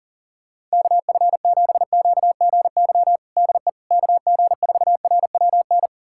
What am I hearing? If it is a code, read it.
Morse code, heard as a keyed tone, about 40 wpm, 700 Hz. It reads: KF7QGY DE KG4RWN
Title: USA Callsign DE Callsign - 6 Characters - 40wpm